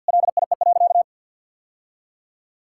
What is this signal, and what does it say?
Morse code, heard as a keyed tone, about 50 wpm, 700 Hz. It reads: LIE.
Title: Sentences from Top 1600 Words 50wpm